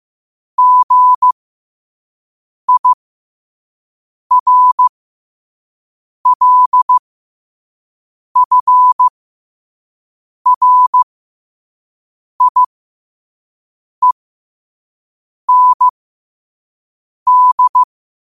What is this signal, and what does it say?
Morse code, heard as a keyed tone, about 15 wpm, spaced as for 5 wpm, 1000 Hz. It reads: GIRLFRIEND
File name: qotdfiles / 5wpm